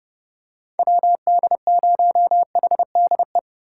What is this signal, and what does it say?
Morse code, heard as a keyed tone, about 30 wpm, 700 Hz. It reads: WD0HDE
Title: USA Call Signs - 6 Characters - 4x Character Spacing - 30wpm